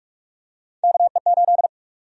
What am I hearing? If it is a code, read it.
Morse code, heard as a keyed tone, about 46 wpm, 700 Hz. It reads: KE8
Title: USA Call Sign Prefixes - 45wpm